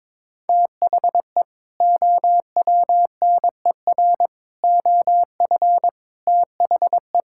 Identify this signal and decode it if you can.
Morse code, heard as a keyed tone, about 21 wpm, 700 Hz. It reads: THE OWNER OF THE
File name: practice / 22wpm / Sentences with 3-Word Phrases - 22wpm